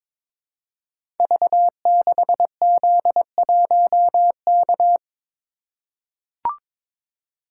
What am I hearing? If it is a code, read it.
Morse code, heard as a keyed tone, about 22 wpm, 700 Hz. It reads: V6Z1K
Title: Code Groups - 5 Letter:Numbers - 22wpm